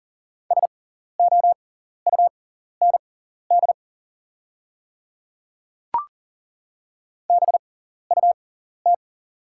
Morse code, heard as a keyed tone, about 40 wpm, spaced as for 14 wpm, 690 Hz. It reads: SOUND BUT
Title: 6x Character Spacing - Top 200 Words - 40wpm